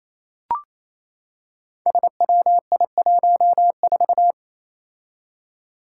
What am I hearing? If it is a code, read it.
Morse code, heard as a keyed tone, about 29 wpm, 700 Hz. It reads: SWI14
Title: Five Challenging Letter-Number Combinations 28wpm